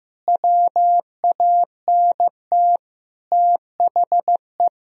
Morse code, heard as a keyed tone, about 15 wpm, 700 Hz. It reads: WANT THE
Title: Sentences from Top 400 Words and Abbreviations 15wpm